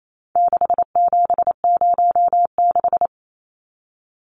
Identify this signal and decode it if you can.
Morse code, heard as a keyed tone, about 28 wpm, 700 Hz. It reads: 6706
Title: Numbers - 4-Digit 28wpm